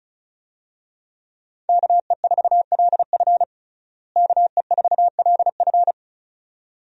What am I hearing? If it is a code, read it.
Morse code, heard as a keyed tone, about 36 wpm, 700 Hz. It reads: KE4LF KE4LF
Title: USA Call Signs - 5 Characters - Repeated 2x - 35wpm